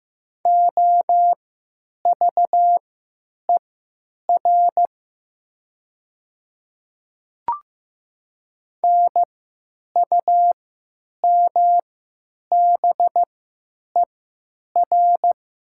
Morse code, heard as a keyed tone, about 15 wpm, spaced as for 9 wpm, 700 Hz. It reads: OVER NUMBER
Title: 3x Character Spacing - Top 200 Words - 15wpm